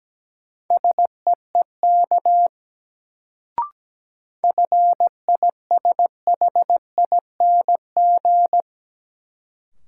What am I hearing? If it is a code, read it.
Morse code, heard as a keyed tone, about 17 wpm, 700 Hz. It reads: SEEK FISHING
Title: Sentences from Top 1600 Words - New Word Drill Alt - 17wpm